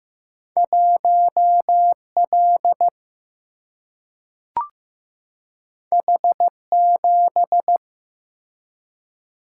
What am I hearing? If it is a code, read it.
Morse code, heard as a keyed tone, about 15 wpm, 700 Hz. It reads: 1L H7
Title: Single-Letter Single-Number 15wpm